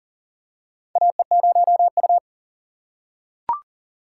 Morse code, heard as a keyed tone, about 40 wpm, 700 Hz. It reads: AE0U
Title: USA Call Signs - 4 Characters - 4x Character Spacing - 40wpm